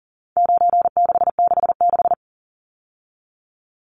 Morse code, heard as a keyed tone, about 38 wpm, 700 Hz. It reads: 9666
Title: Numbers - 4-Digit 40wpm